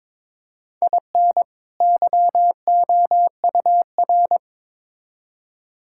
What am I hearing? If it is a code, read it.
Morse code, heard as a keyed tone, about 22 wpm, 700 Hz. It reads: IN YOUR
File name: Build A Phrase - 2-Word Phrases - 22wpm